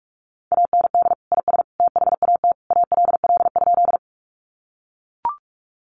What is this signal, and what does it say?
Morse code, heard as a keyed tone, about 44 wpm, 700 Hz. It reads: AND IS THAT ALL?
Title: Sentences from Top 100 Words 45wpm